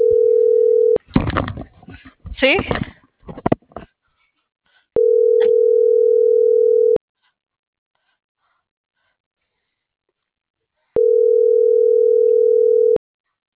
Ringback tone (North America), 440 + 480 Hz.